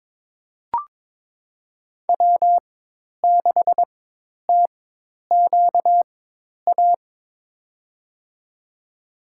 Morse code, heard as a keyed tone, about 22 wpm, spaced as for 10 wpm, 700 Hz. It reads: W6TQA
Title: USA Call Signs - 5 Characters - 4x Character Spacing - 22wpm